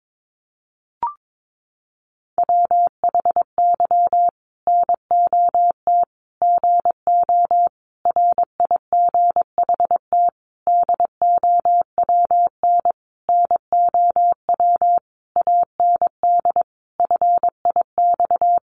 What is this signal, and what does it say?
Morse code, heard as a keyed tone, about 22 wpm, 700 Hz. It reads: WHY NOT GO RIGHT DOWN NOW AND FIX
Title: Sentences from Top 1400 Words 22wpm